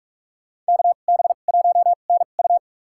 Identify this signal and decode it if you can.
Morse code, heard as a keyed tone, about 46 wpm, spaced as for 33 wpm, 700 Hz. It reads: KB1NU
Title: USA Call Signs - 2x Character Spacing - 45wpm